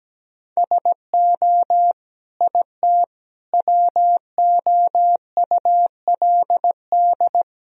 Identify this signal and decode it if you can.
Morse code, heard as a keyed tone, about 17 wpm, 700 Hz. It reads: SO IT WOULD
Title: Sentences from Top 1000 Words - Review Part 1 - 17wpm